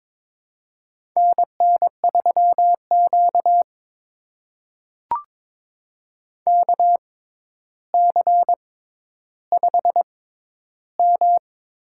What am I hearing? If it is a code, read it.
Morse code, heard as a keyed tone, about 22 wpm, spaced as for 11 wpm, 700 Hz. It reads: NN3Q KC5M
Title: USA Call Signs - 4 Characters - 6x Character Spacing - 22wpm